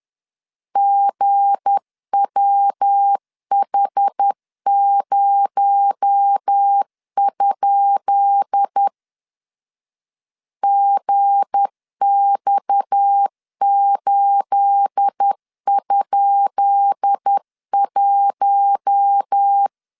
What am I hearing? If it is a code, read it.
Morse code, heard as a keyed tone, about 10 wpm, 800 Hz. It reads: GWH0? GX8?1
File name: LEZIONI / 10 SIGNS / 10B INT 10wpm 2844